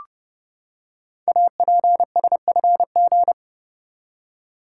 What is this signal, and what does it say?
Morse code, heard as a keyed tone, about 30 wpm, 700 Hz. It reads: APSFG
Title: Code Groups - 5 Letters - 30wpm